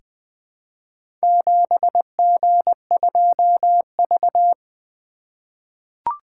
Morse code, heard as a keyed tone, about 20 wpm, 700 Hz. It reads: 7G2V
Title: Four Letter-Number Combinations - 20wpm